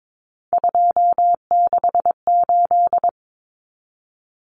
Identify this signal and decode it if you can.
Morse code, heard as a keyed tone, about 22 wpm, 700 Hz. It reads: 268
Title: Common Cut Numbers - 3-Digit 22wpm